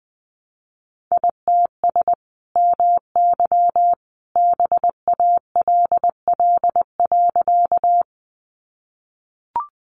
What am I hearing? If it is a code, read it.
Morse code, heard as a keyed tone, about 20 wpm, 700 Hz. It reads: ITS MY BALL.
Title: Sentences from Top 900 Words 20wpm